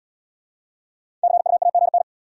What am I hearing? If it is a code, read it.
Morse code, heard as a keyed tone, about 69 wpm, 700 Hz. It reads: CUAGN